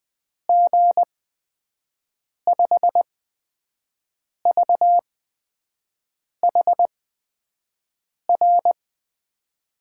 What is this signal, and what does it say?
Morse code, heard as a keyed tone, about 20 wpm, spaced as for 5 wpm, 700 Hz. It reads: G5VHR
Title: USA Call Signs - 6 Characters - 8x Character Spacing - 20wpm